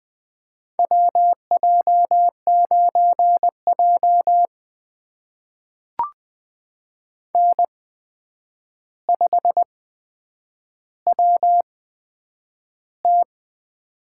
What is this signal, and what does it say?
Morse code, heard as a keyed tone, about 20 wpm, spaced as for 9 wpm, 700 Hz. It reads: WJ9J N5WT
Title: USA Call Signs - 4 Characters - 8x Character Spacing - 20wpm